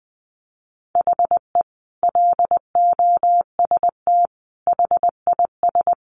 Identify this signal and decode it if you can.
Morse code, heard as a keyed tone, about 19 wpm, 700 Hz. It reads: HE LOST HIS
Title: Sentences from Top 1400 Words 20wpm